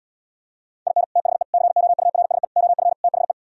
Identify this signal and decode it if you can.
Morse code, heard as a keyed tone, about 78 wpm, 700 Hz. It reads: IN THE COURSE OF THE